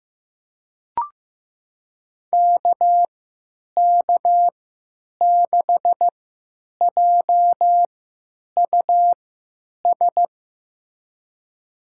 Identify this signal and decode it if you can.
Morse code, heard as a keyed tone, about 15 wpm, spaced as for 9 wpm, 700 Hz. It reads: KK6JUS
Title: USA Call Signs - 6 Characters - 3x Character Spacing - 15wpm